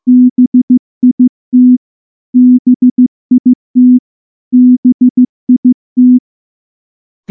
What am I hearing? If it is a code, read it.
Morse code, heard as a keyed tone, about 14 wpm, 250 Hz. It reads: BIT BIT BIT
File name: morse / morse-training-15WPM